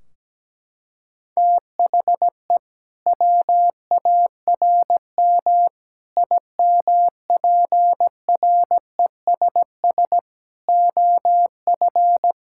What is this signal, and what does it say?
Morse code, heard as a keyed tone, about 17 wpm, 700 Hz. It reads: THE WARM IMPRESS OF
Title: Sentences from Top 1600 Words 17wpm